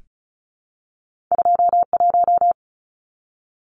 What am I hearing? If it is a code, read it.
Morse code, heard as a keyed tone, about 36 wpm, 700 Hz. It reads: 21
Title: Numbers - 2-Digit 35wpm